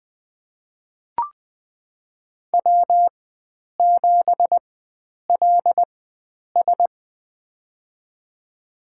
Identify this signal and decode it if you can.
Morse code, heard as a keyed tone, about 20 wpm, spaced as for 9 wpm, 700 Hz. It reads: W7LS
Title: USA Call Signs - 4 Characters - 4x Character Spacing - 20wpm